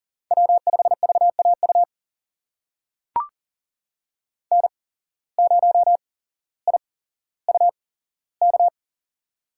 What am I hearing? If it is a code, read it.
Morse code, heard as a keyed tone, about 42 wpm, spaced as for 17 wpm, 700 Hz. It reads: W5VAU N0IUK
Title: USA Call Signs - 5 Characters - 8x Character Spacing - 40wpm